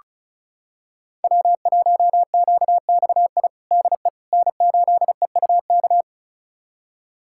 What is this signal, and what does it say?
Morse code, heard as a keyed tone, about 36 wpm, 700 Hz. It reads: W1QXI DE N8EUK
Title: USA Callsign DE Callsign - 5 Characters - 35wpm